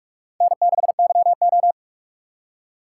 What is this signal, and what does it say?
Morse code, heard as a keyed tone, about 42 wpm, 700 Hz. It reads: N6YO
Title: USA Call Signs - 4 Characters - 2x Character Spacing - 45wpm